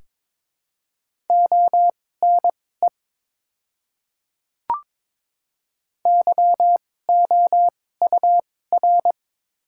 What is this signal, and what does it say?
Morse code, heard as a keyed tone, about 22 wpm, spaced as for 16 wpm, 700 Hz. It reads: ONE YOUR
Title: 2x Character Spacing - Top 100 Words - 22wpm